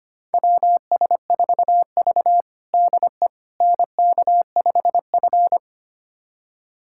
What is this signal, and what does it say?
Morse code, heard as a keyed tone, about 25 wpm, 700 Hz. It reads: WS4V DE NK5F
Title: USA Callsign DE Callsign - 4 Characters - 25wpm